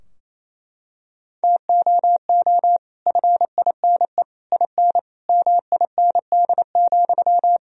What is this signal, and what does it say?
Morse code, heard as a keyed tone, about 28 wpm, 700 Hz. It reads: TOO FINE IN MIND,
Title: Sentences from Top 1000 Words - Review Part 4 - 28wpm